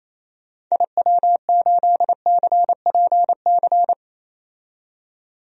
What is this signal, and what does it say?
Morse code, heard as a keyed tone, about 28 wpm, 700 Hz. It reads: IW8CPC
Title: DXCC-100 Call Signs - 6 Characters - 28wpm